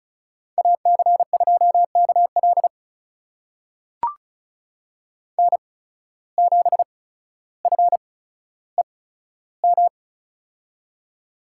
Morse code, heard as a keyed tone, about 36 wpm, spaced as for 15 wpm, 700 Hz. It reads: AC2KL N7FEM
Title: USA Call Signs - 5 Characters - 8x Character Spacing - 35wpm